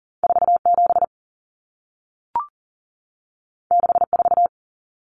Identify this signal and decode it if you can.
Morse code, heard as a keyed tone, about 40 wpm, 700 Hz. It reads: 47 64
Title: Numbers - 2-Digit 40wpm